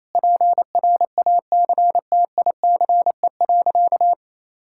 Morse code, heard as a keyed tone, about 28 wpm, 700 Hz. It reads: PRACTICE.
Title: Sentences from Top 1000 Words - Review Part 5 - 28wpm